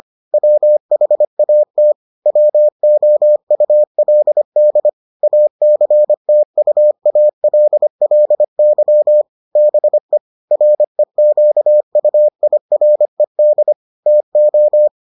Morse code, heard as a keyed tone, about 25 wpm, 600 Hz. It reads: WHAT WOULD ACTUALLY BE REQUIRED TO